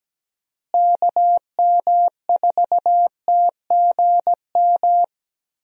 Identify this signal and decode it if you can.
Morse code, heard as a keyed tone, about 17 wpm, 700 Hz. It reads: KM4TGM
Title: USA Call Signs - 6 Characters - Repeated 2x - 17wpm